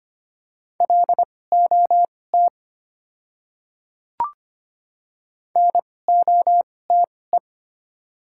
Morse code, heard as a keyed tone, about 25 wpm, spaced as for 18 wpm, 700 Hz. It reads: LOT NOTE
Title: 2x Character Spacing - Top 500 Words - 25wpm